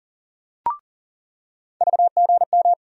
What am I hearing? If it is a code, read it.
Morse code, heard as a keyed tone, about 40 wpm, 700 Hz. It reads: VGM
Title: USA Call Sign Postfixes - 3 Letters 40wpm